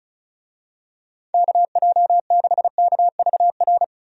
Morse code, heard as a keyed tone, about 36 wpm, 700 Hz. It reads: KJ6KVR
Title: USA Call Signs - 6 Characters - Repeated 2x - 35wpm